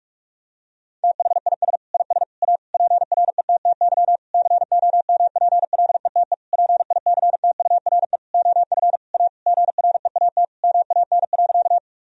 Morse code, heard as a keyed tone, about 43 wpm, 700 Hz. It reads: THIS IS A PRETTY COMPLETE PICTURE OF A GREAT MAN.
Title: Sentences from Top 1000 Words - Review Part 2 - 45wpm